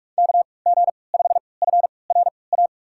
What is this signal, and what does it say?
Morse code, heard as a keyed tone, about 46 wpm, spaced as for 25 wpm, 700 Hz. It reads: KG5FRA
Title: USA Call Signs - 6 Characters - 3x Character Spacing - 45wpm